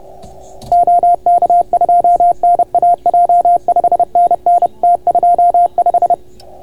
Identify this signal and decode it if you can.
Morse code, heard as a keyed tone, about 31 wpm, 690 Hz. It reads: OK2NAJ 5NN T25